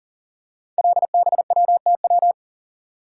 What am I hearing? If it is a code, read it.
Morse code, heard as a keyed tone, about 40 wpm, 700 Hz. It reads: LBWTW
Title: Code Groups - 5 Letter:Numbers - 40wpm